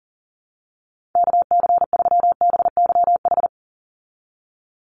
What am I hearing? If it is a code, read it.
Morse code, heard as a keyed tone, about 40 wpm, 690 Hz. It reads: KC3BYH
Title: USA Call Signs 40wpm